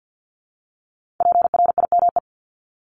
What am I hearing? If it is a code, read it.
Morse code, heard as a keyed tone, about 50 wpm, 700 Hz. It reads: PRIME
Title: Multi-letter Phonemes 50wpm